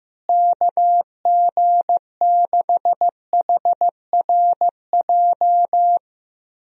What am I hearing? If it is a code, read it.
Morse code, heard as a keyed tone, about 15 wpm, 700 Hz. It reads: KG6HRJ
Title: USA Call Signs - 6 Characters - Repeated 3x - 15wpm